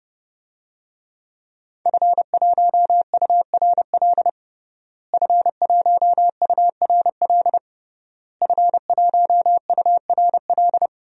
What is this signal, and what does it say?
Morse code, heard as a keyed tone, about 30 wpm, 700 Hz. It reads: F1URL F1URL F1URL
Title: DXCC-100 Call Signs - 5 Characters - Repeated 3x - 30wpm